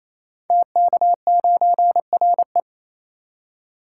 Morse code, heard as a keyed tone, about 28 wpm, 700 Hz. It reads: TK9RE
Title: Code Groups - 5 Letter:Numbers - 28wpm